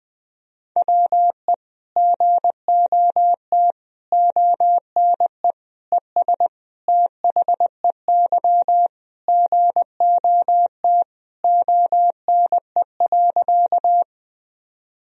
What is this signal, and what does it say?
Morse code, heard as a keyed tone, about 21 wpm, 700 Hz. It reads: WE GOT ONE ES THEY GOT ONE.
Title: Sentences from Top 300 Words and Abbreviations 20wpm